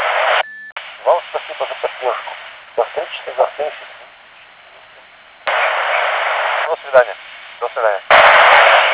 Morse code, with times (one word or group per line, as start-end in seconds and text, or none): none